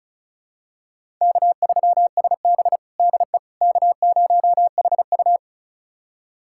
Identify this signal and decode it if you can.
Morse code, heard as a keyed tone, about 36 wpm, 700 Hz. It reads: K3SB DE K0HU
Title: USA Callsign DE Callsign - 4 Characters - 35wpm